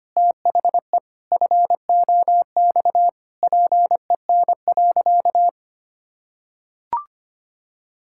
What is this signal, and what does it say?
Morse code, heard as a keyed tone, about 25 wpm, 700 Hz. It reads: THE FOX PEN.
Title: Pangram Challenge v2 - 25wpm